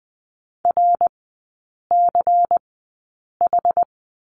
Morse code, heard as a keyed tone, about 19 wpm, spaced as for 8 wpm, 700 Hz. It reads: RCH
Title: 127 Lesson 21e - Number 6 - Three Characters 20wpm